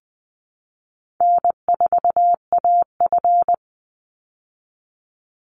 Morse code, heard as a keyed tone, about 20 wpm, 700 Hz. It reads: N4AF
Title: Easy Call Signs - 20wpm